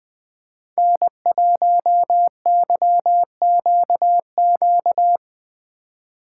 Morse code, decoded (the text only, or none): N1YQQ